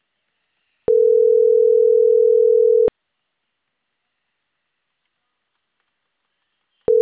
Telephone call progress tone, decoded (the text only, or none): ringback tone